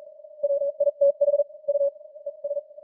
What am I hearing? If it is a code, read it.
Morse code, heard as a keyed tone, about 41 wpm, spaced as for 36 wpm, 570 Hz. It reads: WITH U ES